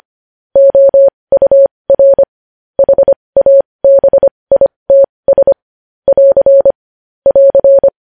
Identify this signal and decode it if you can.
Morse code, heard as a keyed tone, about 25 wpm, spaced as for 20 wpm, 550 Hz. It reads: OUR HABITS <AR> <AR>
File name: podcast / News headlines-1x-25x20